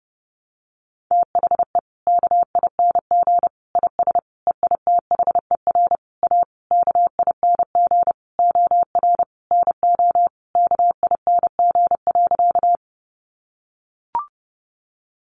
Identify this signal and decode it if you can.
Morse code, heard as a keyed tone, about 30 wpm, 700 Hz. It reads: THE KING IS EITHER A KING OR NO KING.
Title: Sentences from Top 600 Words 30wpm